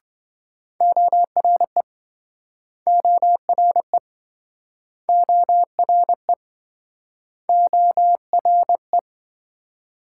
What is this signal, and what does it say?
Morse code, heard as a keyed tone, about 29 wpm, 700 Hz. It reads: ORE ORE ORE ORE